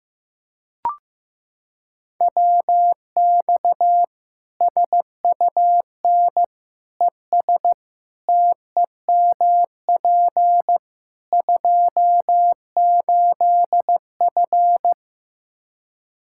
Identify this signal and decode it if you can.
Morse code, heard as a keyed tone, about 15 wpm, 700 Hz. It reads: WX SUN ES TEMP 28F
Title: Practice QSO - Exchange 2 - Info - 15wpm